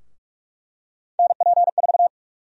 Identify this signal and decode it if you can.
Morse code, heard as a keyed tone, about 47 wpm, 700 Hz. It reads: NP4
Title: USA Call Sign Prefixes - Repeated 2x - 45wpm